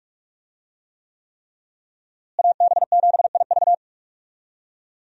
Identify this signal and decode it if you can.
Morse code, heard as a keyed tone, about 46 wpm, 700 Hz. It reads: AB7IV
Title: USA Call Signs - 5 Characters - Repeated 3x - 45wpm